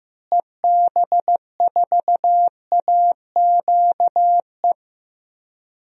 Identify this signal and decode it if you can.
Morse code, heard as a keyed tone, about 15 wpm, 700 Hz. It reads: EB4AQE
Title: DXCC-100 Call Signs - 6 Characters - Repeated 2x - 15wpm